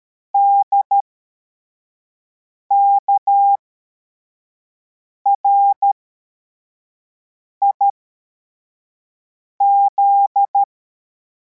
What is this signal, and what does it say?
Morse code, heard as a keyed tone, about 13 wpm, spaced as for 4 wpm, 800 Hz. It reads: DKRIZ
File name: formation / cw / Text74